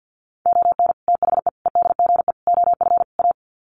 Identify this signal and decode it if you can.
Morse code, heard as a keyed tone, about 49 wpm, 700 Hz. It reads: ON THE EDGE OF A
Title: Sets of 5 Words Encore2 50wpm